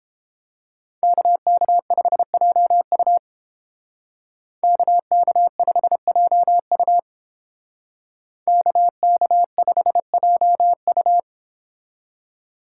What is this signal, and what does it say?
Morse code, heard as a keyed tone, about 30 wpm, 700 Hz. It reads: KK5JU KK5JU KK5JU